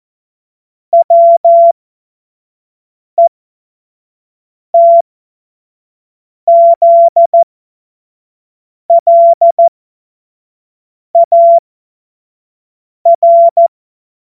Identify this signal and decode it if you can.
Morse code, heard as a keyed tone, about 14 wpm, spaced as for 5 wpm, 680 Hz. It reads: WETZLAR
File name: Morsekurs / Witze / Witz 08. November 2022 25 F70